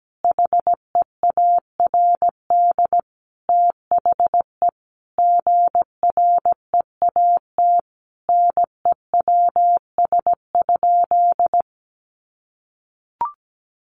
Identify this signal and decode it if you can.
Morse code, heard as a keyed tone, about 17 wpm, 700 Hz. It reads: HEARD THE GREAT NEWS?
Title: Sentences from Top 800 Words 17wpm